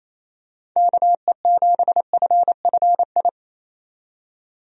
Morse code, heard as a keyed tone, about 28 wpm, 700 Hz. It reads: KE7FFI
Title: USA Call Signs - 6 Characters - 28wpm